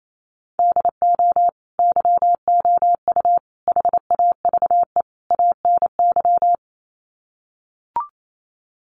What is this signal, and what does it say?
Morse code, heard as a keyed tone, about 28 wpm, 700 Hz. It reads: DO YOU HAVE ANY
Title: Sets of 4 Words 28wpm